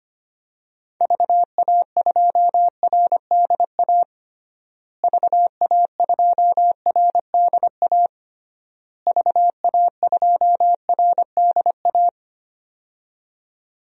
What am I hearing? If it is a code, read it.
Morse code, heard as a keyed tone, about 25 wpm, 700 Hz. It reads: VA2RDA VA2RDA VA2RDA
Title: DXCC-100 Call Signs - 6 Characters - Repeated 3x - 25wpm